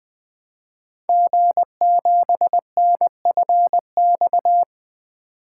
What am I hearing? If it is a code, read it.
Morse code, heard as a keyed tone, about 20 wpm, 700 Hz. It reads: G7NFX
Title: Code Groups - 5 Letter:Numbers - 20wpm